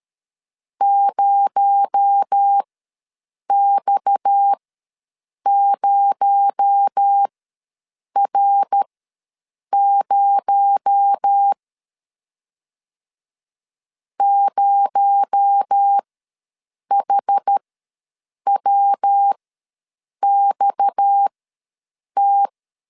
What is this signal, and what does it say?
Morse code, heard as a keyed tone, about 13 wpm, spaced as for 7 wpm, 800 Hz. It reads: 0X0R0 0HWXT
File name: LEZIONI / 6 PKX0 / 6 PKX0 ws 12wpm 3448